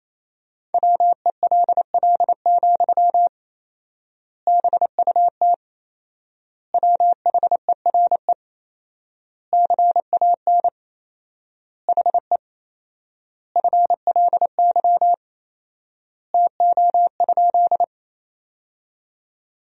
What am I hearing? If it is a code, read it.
Morse code, heard as a keyed tone, about 27 wpm, 700 Hz. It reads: WELL, BUT WHERE CAN HE FLY TO?
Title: 4x Word-Spacing - Sentences from Top 500 Words - 28wpm